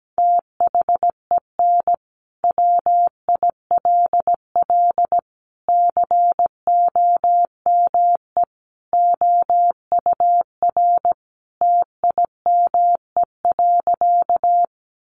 Morse code, heard as a keyed tone, about 17 wpm, 700 Hz. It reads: THEN WILL COME OUR TIME.
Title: Sentences from Top 200 Words 17wpm